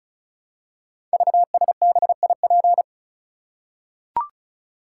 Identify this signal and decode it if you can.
Morse code, heard as a keyed tone, about 36 wpm, 700 Hz. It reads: VSBIP
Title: Code Groups - 5 Letter:Numbers - 35wpm